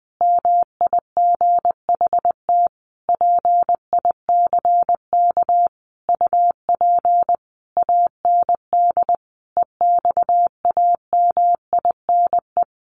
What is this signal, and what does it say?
Morse code, heard as a keyed tone, about 20 wpm, 700 Hz. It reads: MIGHT PICK UP AND EXAMINE
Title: A Sound Of Thunder 20wpm